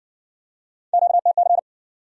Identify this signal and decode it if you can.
Morse code, heard as a keyed tone, about 56 wpm, 700 Hz. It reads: QTC